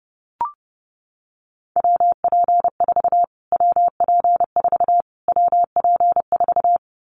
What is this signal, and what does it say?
Morse code, heard as a keyed tone, about 28 wpm, 690 Hz. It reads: WP4 WP4 WP4
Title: Callsign Prefixes for Top 25 DXCC - Repeated 3x - 30wpm